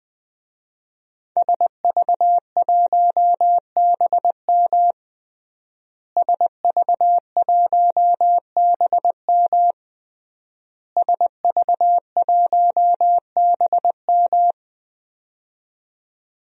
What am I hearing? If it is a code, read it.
Morse code, heard as a keyed tone, about 20 wpm, 700 Hz. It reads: SV1BM SV1BM SV1BM
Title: DXCC-100 Call Signs - 5 Characters - Repeated 3x - 20wpm